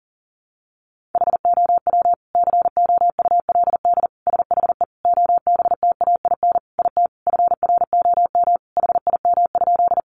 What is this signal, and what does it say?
Morse code, heard as a keyed tone, about 40 wpm, 700 Hz. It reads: HOW COULD SHE OBTAIN IT FROM HIM?
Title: Sentences from Top 1300 Words 40wpm